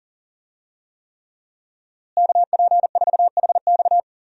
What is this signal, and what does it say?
Morse code, heard as a keyed tone, about 40 wpm, 700 Hz. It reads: KP4HX
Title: USA Call Signs - 5 Characters - Repeated 3x - 40wpm